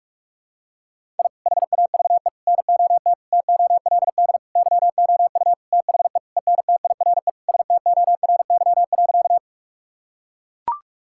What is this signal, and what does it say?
Morse code, heard as a keyed tone, about 41 wpm, 700 Hz. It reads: I HAVE NOT TOLD YOU THE ENTIRE STORY.
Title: Sentences from Top 1000 Words - Review Part 5 - 45wpm